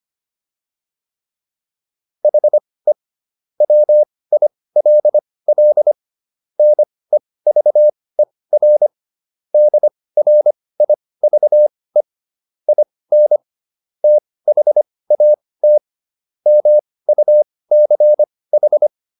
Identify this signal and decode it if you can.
Morse code, heard as a keyed tone, about 25 wpm, spaced as for 18 wpm, 600 Hz. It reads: HE WILL NEVER DRIVE IN THAT MUCH